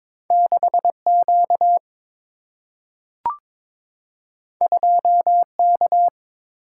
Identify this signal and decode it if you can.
Morse code, heard as a keyed tone, about 22 wpm, 700 Hz. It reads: 6 Q 2 K
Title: Single-Letter Single-Number 22wpm